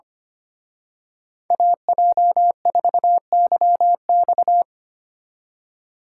Morse code, heard as a keyed tone, about 31 wpm, 700 Hz. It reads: AJ4YX
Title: Speed Racing - USA 2x2 Call Signs 25wpm